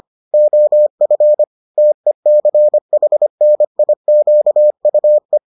Morse code, heard as a keyed tone, about 25 wpm, 600 Hz. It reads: OF TECHNIQUE